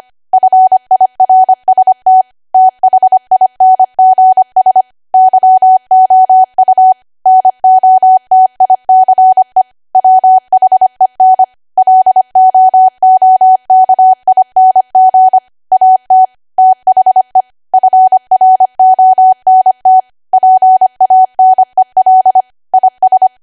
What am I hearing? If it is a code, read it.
Morse code, heard as a keyed tone, about 25 wpm, 750 Hz. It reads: FIRST THINGS YOU NOTICE WHEN LOOKING AT THE FRONT PANEL IS